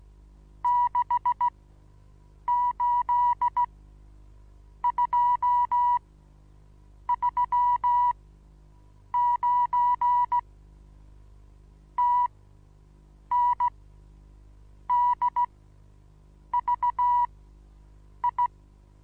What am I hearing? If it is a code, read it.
Morse code, heard as a keyed tone, about 16 wpm, spaced as for 7 wpm, 970 Hz. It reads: 68239 TNDVI
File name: Onlinelehrgang / morsen / DJ4UF Morsen25